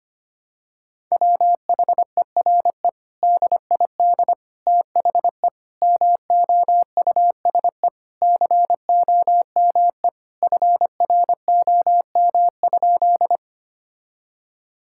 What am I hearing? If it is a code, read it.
Morse code, heard as a keyed tone, about 25 wpm, 700 Hz. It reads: WHERE DID THE MOUSE COME FROM?
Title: Sentences from Top 1000 Words - Review Part 5 - 25wpm